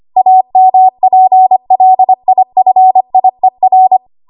Morse code, heard as a keyed tone, about 25 wpm, 750 Hz. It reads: AMPLIFIER
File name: Archive / 25 WPM / 25WPM